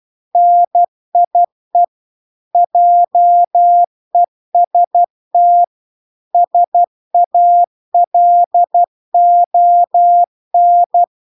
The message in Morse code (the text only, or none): NIE JEST SALON